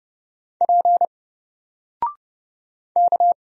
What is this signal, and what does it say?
Morse code, heard as a keyed tone, about 30 wpm, 700 Hz. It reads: P K
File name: Single-Letter-Number-30wpm